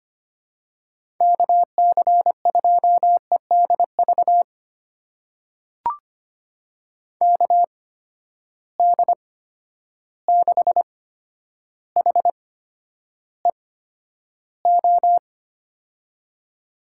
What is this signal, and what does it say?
Morse code, heard as a keyed tone, about 25 wpm, spaced as for 11 wpm, 700 Hz. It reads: KC2EDV KD6HEO
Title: USA Call Signs - 6 Characters - 8x Character Spacing - 25wpm